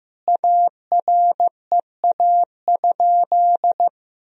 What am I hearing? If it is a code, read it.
Morse code, heard as a keyed tone, about 15 wpm, 700 Hz. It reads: AREA?
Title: Sentences from Top 1000 Words - Review Part 3 - 15wpm